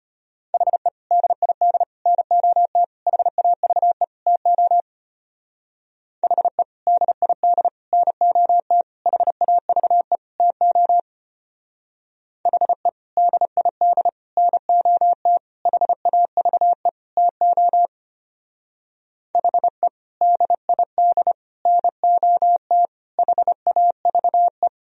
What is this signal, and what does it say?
Morse code, heard as a keyed tone, about 29 wpm, 700 Hz. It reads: HE DID NOT HAVE TO HE DID NOT HAVE TO HE DID NOT HAVE TO HE DID NOT HAVE
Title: Speed-Racing - Sets of 5 Words 25wpm